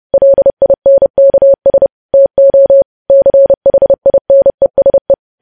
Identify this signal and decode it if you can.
Morse code, heard as a keyed tone, about 30 wpm, 550 Hz. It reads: LINKS TO CHINESE
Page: news headlines in morse code at 30 wpm